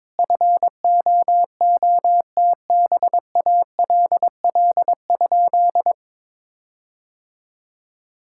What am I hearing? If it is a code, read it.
Morse code, heard as a keyed tone, about 22 wpm, 700 Hz. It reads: FOOTBALL?
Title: Sentences from Top 1600 Words 22wpm